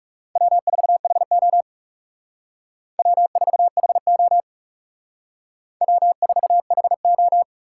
Morse code, heard as a keyed tone, about 37 wpm, 700 Hz. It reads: W4HO W4HO W4HO
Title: Speed Racing - USA 1x2 Call Signs 30wpm